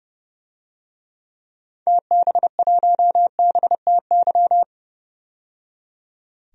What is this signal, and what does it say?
Morse code, heard as a keyed tone, about 31 wpm, 700 Hz. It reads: TB1BTY
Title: DXCC-100 Call Signs - 6 Characters - 30wpm